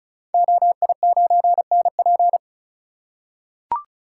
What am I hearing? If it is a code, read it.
Morse code, heard as a keyed tone, about 36 wpm, 690 Hz. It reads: OI9NP
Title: Code Groups - 5 Letter:Numbers - 35wpm